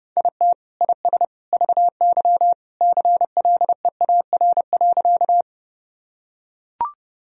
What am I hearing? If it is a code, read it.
Morse code, heard as a keyed tone, about 30 wpm, 700 Hz. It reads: IT IS VY CLEAR.
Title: Sentences from Top 500 Words and Abbreviations 30wpm